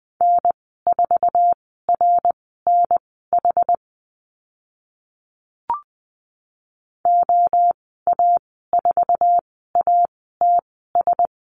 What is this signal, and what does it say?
Morse code, heard as a keyed tone, about 20 wpm, spaced as for 14 wpm, 700 Hz. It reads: N4RNH OA4ATS
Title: 082 Lesson 14g - Number 2 - Call Signs 20wpm